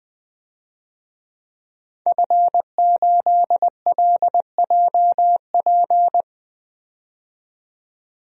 Morse code, heard as a keyed tone, about 20 wpm, 700 Hz. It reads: F8LJP
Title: DXCC-100 Call Signs - 5 Characters - 20wpm